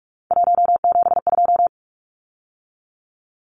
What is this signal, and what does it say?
Morse code, heard as a keyed tone, about 44 wpm, 700 Hz. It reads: 172
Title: Numbers - 3-Digit 45wpm